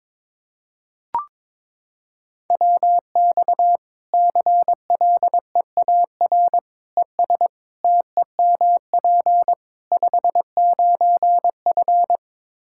Morse code, heard as a keyed tone, about 22 wpm, 700 Hz. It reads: WX CLEAR ES TEMP 59F
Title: K1USN SST QSO Practice 22wpm